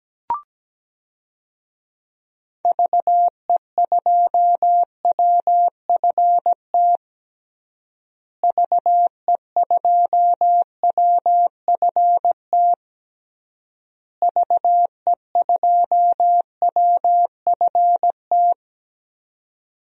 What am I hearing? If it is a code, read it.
Morse code, heard as a keyed tone, about 17 wpm, 700 Hz. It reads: VE2WFT VE2WFT VE2WFT
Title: DXCC-100 Call Signs - 6 Characters - Repeated 3x - 17wpm